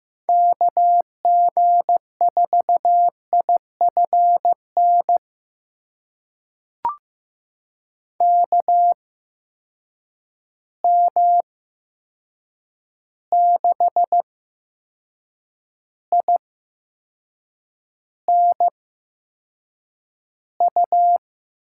Morse code, heard as a keyed tone, about 15 wpm, spaced as for 6 wpm, 700 Hz. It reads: KG4IFN KM6INU
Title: USA Call Signs - 6 Characters - 8x Character Spacing - 15wpm